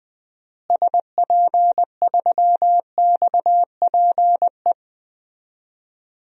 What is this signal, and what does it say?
Morse code, heard as a keyed tone, about 20 wpm, 700 Hz. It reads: SP3XPE
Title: DXCC-100 Call Signs - 6 Characters - Repeated 2x - 20wpm